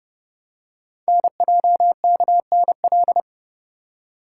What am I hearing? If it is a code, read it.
Morse code, heard as a keyed tone, about 29 wpm, 700 Hz. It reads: NJKNL
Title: Code Groups - 5 Letter:Numbers - 30wpm